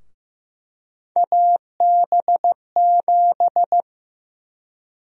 Morse code, heard as a keyed tone, about 15 wpm, 700 Hz. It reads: AB7
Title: USA Call Sign Prefixes - 15wpm